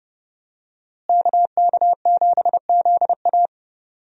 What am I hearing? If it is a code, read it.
Morse code, heard as a keyed tone, about 29 wpm, 700 Hz. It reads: KK7ZA
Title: USA Call Signs - 5 Characters - Repeated 3x - 30wpm